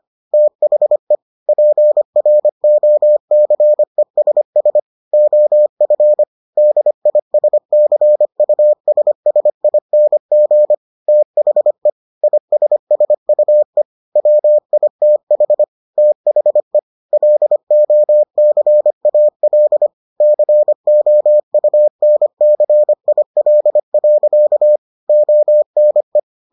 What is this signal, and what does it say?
Morse code, heard as a keyed tone, about 25 wpm, 600 Hz. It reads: THE PROCESS OF DISCUSSING THE ISSUE WITH THE LOCAL COUNCIL. ONE